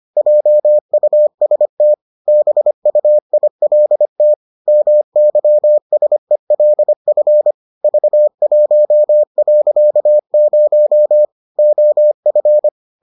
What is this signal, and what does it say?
Morse code, heard as a keyed tone, about 25 wpm, 600 Hz. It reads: JUST BUILT MYSELF V1.0 OF